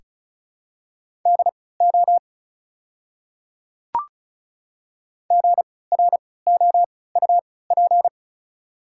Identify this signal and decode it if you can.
Morse code, heard as a keyed tone, about 36 wpm, spaced as for 20 wpm, 690 Hz. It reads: DO GROUP
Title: 3x Character Spacing - Top 500 Words - 35wpm